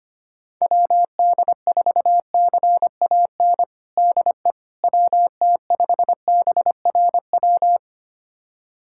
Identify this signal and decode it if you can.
Morse code, heard as a keyed tone, about 25 wpm, 700 Hz. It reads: WD4CAN DE WT5BRW
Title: USA Callsign DE Callsign 25wpm